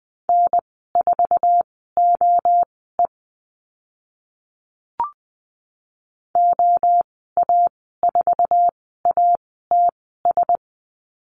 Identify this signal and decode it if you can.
Morse code, heard as a keyed tone, about 20 wpm, spaced as for 14 wpm, 700 Hz. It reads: N4OE OA4ATS
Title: 069 Lesson 12g - Letter d - Call Signs 20wpm